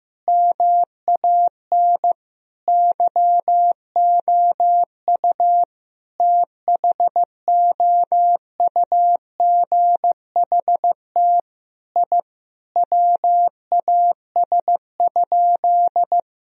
Morse code, heard as a keyed tone, about 15 wpm, 700 Hz. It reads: MAN YOU THOUGHT I WAS?
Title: Sentences from Top 1000 Words - Review Part 3 - 15wpm